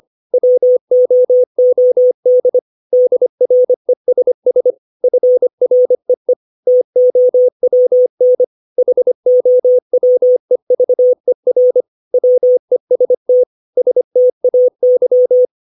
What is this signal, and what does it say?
Morse code, heard as a keyed tone, about 25 wpm, 500 Hz. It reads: WOOD DRESS FREE TOWN HOWEVER WEST STAY
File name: 632 words / 632 Words 25WPM